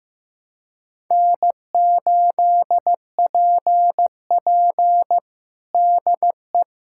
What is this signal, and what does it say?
Morse code, heard as a keyed tone, about 15 wpm, 700 Hz. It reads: N8PP DE
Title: USA Callsign DE Callsign - 4 Characters - 15wpm